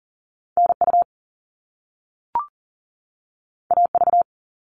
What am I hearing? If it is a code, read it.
Morse code, heard as a keyed tone, about 40 wpm, 700 Hz. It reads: NU AV
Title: All Cut Numbers - 2-Digit 40wpm